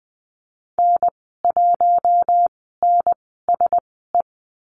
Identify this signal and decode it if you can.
Morse code, heard as a keyed tone, about 20 wpm, spaced as for 15 wpm, 700 Hz. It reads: N1NSE
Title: 069 Lesson 12g - Letter d - Call Signs 20wpm